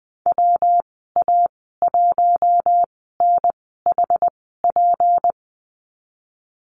Morse code, heard as a keyed tone, about 20 wpm, spaced as for 15 wpm, 700 Hz. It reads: WA1NHP
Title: 173 Lesson 28g - Number 9 - Call Signs 20wpm